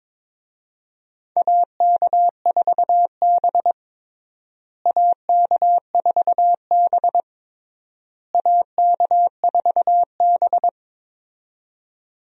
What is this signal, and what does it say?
Morse code, heard as a keyed tone, about 22 wpm, 700 Hz. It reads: AK4B AK4B AK4B
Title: USA Call Signs - 4 Characters - Repeated 3x - 22wpm